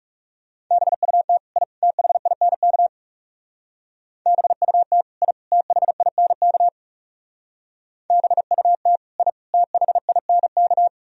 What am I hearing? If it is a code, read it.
Morse code, heard as a keyed tone, about 39 wpm, 700 Hz. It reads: BUT I THINK BUT I THINK BUT I THINK